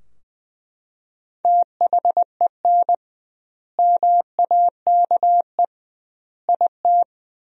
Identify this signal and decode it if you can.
Morse code, heard as a keyed tone, about 21 wpm, 700 Hz. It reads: THEN MAKE IT
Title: 2x Word-Spacing - Sentences from Top 100 Words - 20wpm